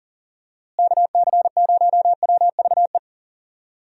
Morse code, heard as a keyed tone, about 38 wpm, 700 Hz. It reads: KC0WVE
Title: USA Call Signs - 6 Characters - Repeated 2x - 40wpm